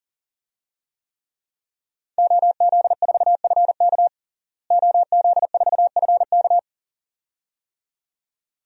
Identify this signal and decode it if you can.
Morse code, heard as a keyed tone, about 40 wpm, 700 Hz. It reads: OZ4FK OZ4FK
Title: DXCC-100 Call Signs - 5 Characters - Repeated 2x - 40wpm